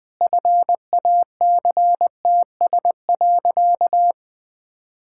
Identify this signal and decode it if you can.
Morse code, heard as a keyed tone, about 20 wpm, 700 Hz. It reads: FACTS.
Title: Sentences with 5-Word Phrases - Spoken Beginning - 20wpm